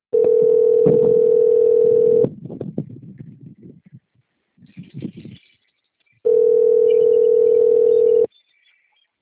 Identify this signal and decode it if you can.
Ringback tone (North America), 440 + 480 Hz.
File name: out- - - - - .22556